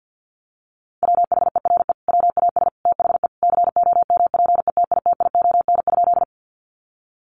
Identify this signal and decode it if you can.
Morse code, heard as a keyed tone, about 50 wpm, 700 Hz. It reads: WHERE WAS THE COMPETITION?
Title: Sentences from Top 1000 Words 50wpm